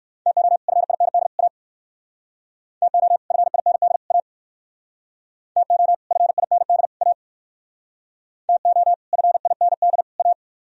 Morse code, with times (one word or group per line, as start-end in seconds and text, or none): 0.3-0.6 TO
0.7-1.3 FIND
1.4-1.5 A
2.8-3.2 TO
3.3-4.0 FIND
4.1-4.2 A
5.6-5.9 TO
6.1-6.9 FIND
7.0-7.1 A
8.5-8.9 TO
9.1-10.0 FIND
10.2-10.3 A